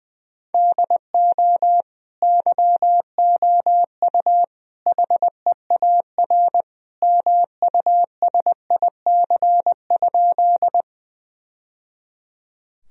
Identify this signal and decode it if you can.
Morse code, heard as a keyed tone, about 20 wpm, 700 Hz. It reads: DO YOU HEAR MUSIC?
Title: Sentences from Top 1000 Words - Review Part 1 - 20wpm